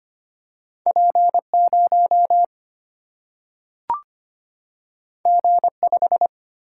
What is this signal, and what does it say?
Morse code, heard as a keyed tone, about 25 wpm, 700 Hz. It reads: P 0 G 5
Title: Single-Letter Single-Number 25wpm